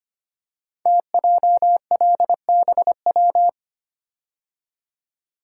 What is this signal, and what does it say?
Morse code, heard as a keyed tone, about 25 wpm, 700 Hz. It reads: TJLBW
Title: Code Groups - 5 Letter:Numbers - 25wpm